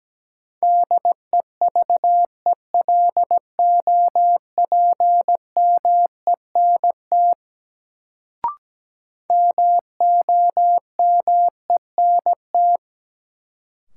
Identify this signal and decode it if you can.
Morse code, heard as a keyed tone, about 17 wpm, 700 Hz. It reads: DEVELOPMENT MOMENT
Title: Sets of 3 Words - New Word Drill - 17wpm